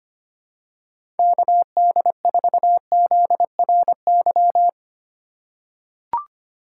Morse code, heard as a keyed tone, about 25 wpm, 700 Hz. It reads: KD4ZRY
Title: USA Call Signs - 4x Character Spacing - 25wpm